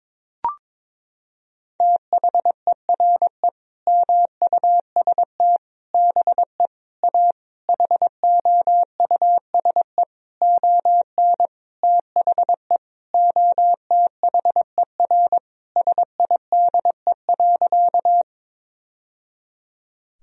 Morse code, heard as a keyed tone, about 22 wpm, 700 Hz. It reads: THERE MUST BE A HOUSE ON THE OTHER SIDE.
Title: Sentences from Top 1000 Words - Review Part 1 - 22wpm